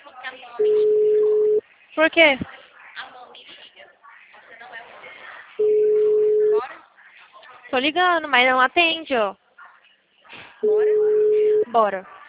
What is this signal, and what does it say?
Ringback tone (Europe, 425 Hz), with speech.
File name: out- -2007- - - .43